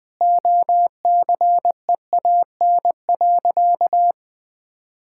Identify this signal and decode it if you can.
Morse code, heard as a keyed tone, about 20 wpm, 700 Hz. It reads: OCEAN.
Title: Sentences with 8-Word Phrases - Spoken Beginning - 20wpm